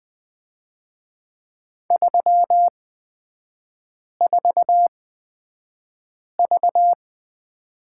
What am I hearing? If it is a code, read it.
Morse code, heard as a keyed tone, about 20 wpm, spaced as for 5 wpm, 690 Hz. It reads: 34V
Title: 236 Lesson 38e - Letter q - Three Characters 20wpm